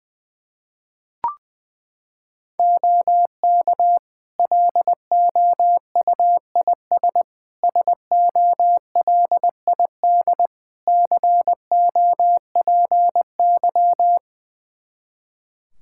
Morse code, heard as a keyed tone, about 21 wpm, 700 Hz. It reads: OK LOUIS SOLID COPY
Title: Practice QSO - Exchange 3 - More Info - 20wpm